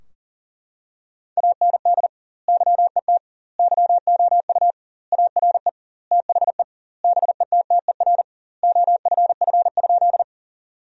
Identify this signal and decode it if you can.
Morse code, heard as a keyed tone, about 40 wpm, 700 Hz. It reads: AND YET YOU ARE THE BETTER OFF?
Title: 2x Word-Spacing - Sentences from Top 500 Words - 40wpm